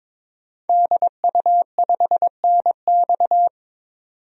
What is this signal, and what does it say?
Morse code, heard as a keyed tone, about 22 wpm, 700 Hz. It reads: DU5NX
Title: DXCC-100 Call Signs - 5 Characters - 22wpm